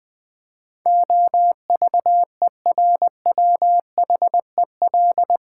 Morse code, heard as a keyed tone, about 20 wpm, 700 Hz. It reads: OVERWHEL
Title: Build A Word - 4 Letters At A Time - 12-Letter Words - 20wpm